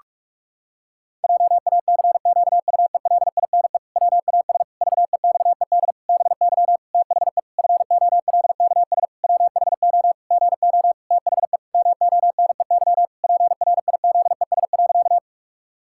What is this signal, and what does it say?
Morse code, heard as a keyed tone, about 46 wpm, 700 Hz. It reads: JACQUELINE WAS VEXED BY THE FOLKS WHO GO THE MONEY PRIZES.
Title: Pangram Challenge v2 - 45wpm